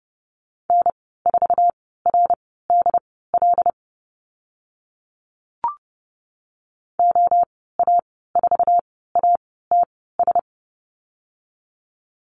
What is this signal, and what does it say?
Morse code, heard as a keyed tone, about 30 wpm, spaced as for 17 wpm, 700 Hz. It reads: N4RDL OA4ATS
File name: Lesson 14g - Number 2 - Call Signs 30wpm